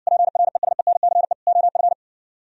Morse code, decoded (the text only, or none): PRESENCE OF